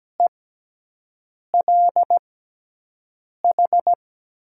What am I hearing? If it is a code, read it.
Morse code, heard as a keyed tone, about 17 wpm, spaced as for 6 wpm, 700 Hz. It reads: ELH
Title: USA Call Signs - 5 Characters - 6x Character Spacing - 17wpm